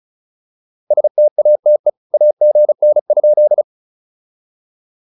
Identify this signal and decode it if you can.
Morse code, heard as a keyed tone, about 34 wpm, 600 Hz. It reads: STATE AGN?
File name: POTA206 35